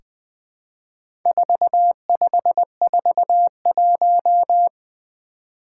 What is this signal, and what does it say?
Morse code, heard as a keyed tone, about 21 wpm, 700 Hz. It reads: 4541
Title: Four Challenging Letter-Number Combinations 20wpm